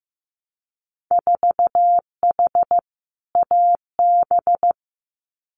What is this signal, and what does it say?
Morse code, heard as a keyed tone, about 15 wpm, 700 Hz. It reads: 4H AB
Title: Winter Field Day Exchanges Encore 15wpm